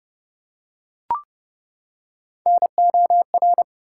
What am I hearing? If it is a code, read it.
Morse code, heard as a keyed tone, about 28 wpm, 700 Hz. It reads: NOR
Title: Three Letter Words - 30wpm